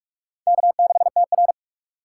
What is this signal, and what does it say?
Morse code, heard as a keyed tone, about 47 wpm, 700 Hz. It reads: K6TR
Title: USA Call Signs - 4x Character Spacing - 45wpm